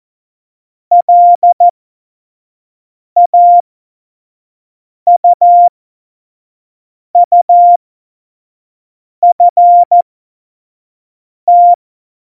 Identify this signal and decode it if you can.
Morse code, heard as a keyed tone, about 14 wpm, spaced as for 5 wpm, 700 Hz. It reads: LAUUFT